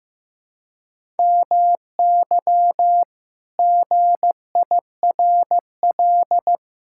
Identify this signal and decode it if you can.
Morse code, heard as a keyed tone, about 15 wpm, 700 Hz. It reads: MY GIRL
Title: Pangram Challenge v2 - 15wpm